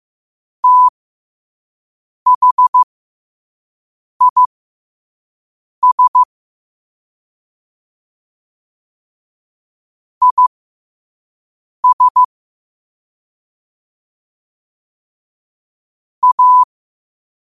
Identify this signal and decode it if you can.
Morse code, heard as a keyed tone, about 15 wpm, spaced as for 5 wpm, 1000 Hz. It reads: THIS IS A
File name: qotdfiles / 5wpm